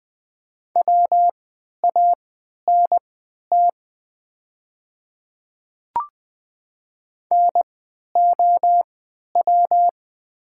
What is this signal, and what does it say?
Morse code, heard as a keyed tone, about 20 wpm, spaced as for 11 wpm, 700 Hz. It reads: WANT NOW
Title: 3x Character Spacing - Top 300 Words - 20wpm